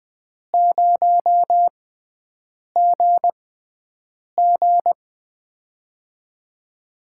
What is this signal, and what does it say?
Morse code, heard as a keyed tone, about 20 wpm, spaced as for 7 wpm, 700 Hz. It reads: 0GG
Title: USA Call Signs - 4 Characters - 6x Character Spacing - 20wpm